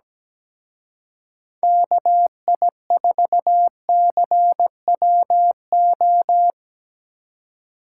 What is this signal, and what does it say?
Morse code, heard as a keyed tone, about 20 wpm, 700 Hz. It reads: KI4CWO
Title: Speed Racing - USA 2x3 Call Signs 17wpm